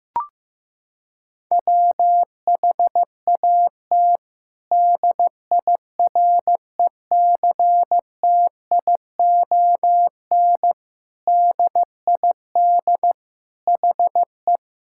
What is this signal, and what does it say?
Morse code, heard as a keyed tone, about 15 wpm, 700 Hz. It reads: WHAT DIRECTION DID HE
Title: Sentences from Top 1000 Words - Review Part 5 - 15wpm